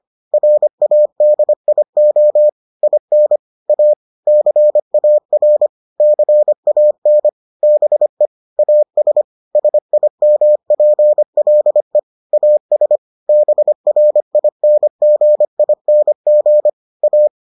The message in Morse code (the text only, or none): RADIO IN A CAR CAN BE AS SIMPLE AS BRINGING A